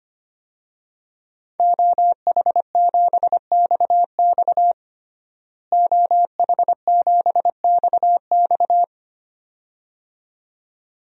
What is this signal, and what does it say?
Morse code, heard as a keyed tone, about 25 wpm, 700 Hz. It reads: OH7XX OH7XX
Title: DXCC-100 Call Signs - 5 Characters - Repeated 2x - 25wpm